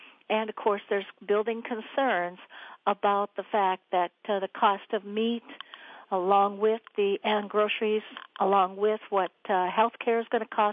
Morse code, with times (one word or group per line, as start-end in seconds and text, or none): none